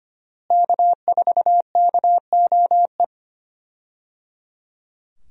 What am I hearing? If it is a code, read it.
Morse code, heard as a keyed tone, about 25 wpm, 700 Hz. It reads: K4KOE
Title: USA Call Signs - 5 Characters - Repeated 3x - 25wpm